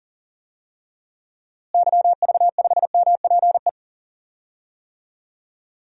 Morse code, heard as a keyed tone, about 40 wpm, 700 Hz. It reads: YV5MPE
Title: DXCC-100 Call Signs - 6 Characters - Repeated 3x - 40wpm